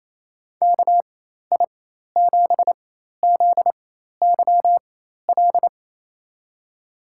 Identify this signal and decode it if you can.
Morse code, heard as a keyed tone, about 28 wpm, spaced as for 13 wpm, 700 Hz. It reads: KI7ZYL
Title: USA Call Signs - 6 Characters - 4x Character Spacing - 28wpm